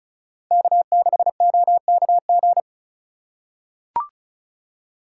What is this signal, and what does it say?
Morse code, heard as a keyed tone, about 36 wpm, 700 Hz. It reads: K6OKG
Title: USA Call Signs - 3x Character Spacing - 35wpm